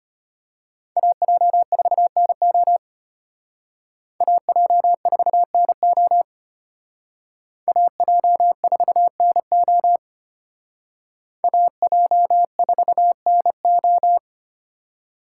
Morse code, heard as a keyed tone, about 32 wpm, 700 Hz. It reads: AJ4NO AJ4NO AJ4NO AJ4NO